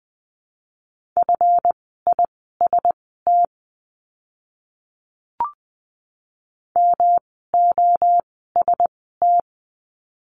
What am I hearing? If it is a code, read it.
Morse code, heard as a keyed tone, about 20 wpm, spaced as for 14 wpm, 700 Hz. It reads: FIST MOST
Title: 139 Lesson 23f - Letter f - Words 20wpm